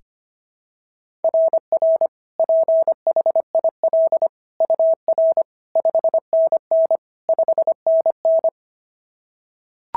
Morse code, heard as a keyed tone, about 25 wpm, 660 Hz. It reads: RR PHIL UR 5NN 5NN
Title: SOTA QSO Practice 25wpm